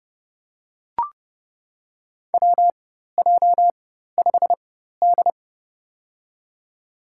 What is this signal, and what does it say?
Morse code, heard as a keyed tone, about 30 wpm, spaced as for 14 wpm, 700 Hz. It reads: WJ5D
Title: USA Call Signs - 4 Characters - 4x Character Spacing - 30wpm